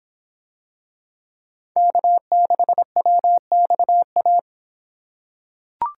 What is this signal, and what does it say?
Morse code, heard as a keyed tone, about 24 wpm, 700 Hz. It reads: K6WXA